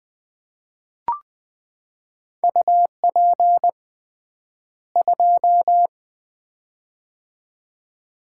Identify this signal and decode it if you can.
Morse code, heard as a keyed tone, about 20 wpm, 700 Hz. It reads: UP 2
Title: Common QSO Phrases - 3x Word-Spacing - 20wpm